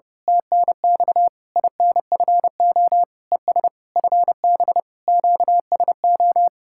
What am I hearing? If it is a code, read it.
Morse code, heard as a keyed tone, about 28 wpm, 700 Hz. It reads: TNX INFO ES FB QSO
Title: Speed-Racing Common QSO Phrases - 30wpm